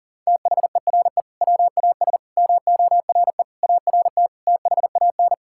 Morse code, heard as a keyed tone, about 40 wpm, 700 Hz. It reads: THERE WAS MORE ART THAN
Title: Sentences from Top 1000 Words - Review Part 6 - 40wpm